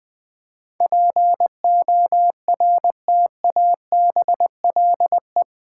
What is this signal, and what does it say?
Morse code, heard as a keyed tone, about 20 wpm, 700 Hz. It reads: PORTABLE
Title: Difficult Words - Speed-Racing with Hints - 20wpm